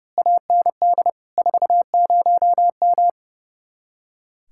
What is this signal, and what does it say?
Morse code, heard as a keyed tone, about 28 wpm, 700 Hz. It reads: AND 40M
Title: Speed-Racing Common QSO Phrases - 30wpm